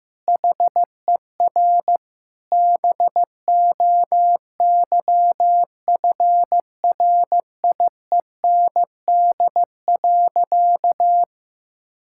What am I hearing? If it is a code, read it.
Morse code, heard as a keyed tone, about 16 wpm, 700 Hz. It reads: HER BOYFRIEND.
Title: Sentences from Top 1000 Words - Review Part 6 - 15wpm